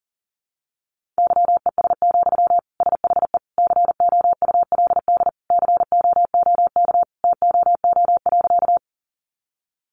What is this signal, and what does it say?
Morse code, heard as a keyed tone, about 40 wpm, 700 Hz. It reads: YES, SHE COULD COOK TOO.
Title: Sentences from Top 700 Words 40wpm